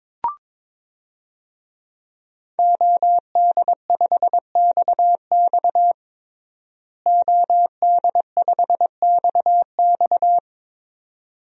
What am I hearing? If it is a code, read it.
Morse code, heard as a keyed tone, about 21 wpm, 700 Hz. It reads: OD5XX OD5XX
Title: DXCC-100 Call Signs - 5 Characters - Repeated 2x - 22wpm